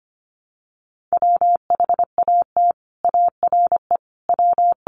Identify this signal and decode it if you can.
Morse code, heard as a keyed tone, about 25 wpm, 700 Hz. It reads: WHAT ARE W
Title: Sets of 5 Words 25wpm